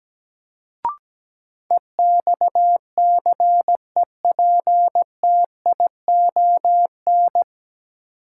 Morse code, heard as a keyed tone, about 17 wpm, 700 Hz. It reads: EXCEPTION
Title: Sets of 4 Words - New Word Drill - 17wpm